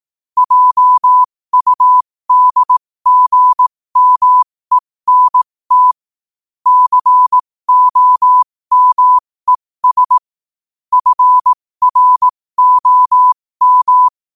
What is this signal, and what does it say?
Morse code, heard as a keyed tone, about 18 wpm, spaced as for 15 wpm, 1000 Hz. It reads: JUDGMENT COMES FROM
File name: qotdfiles / 15wpm